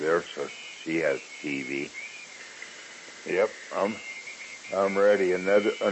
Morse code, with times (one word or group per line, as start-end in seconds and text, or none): none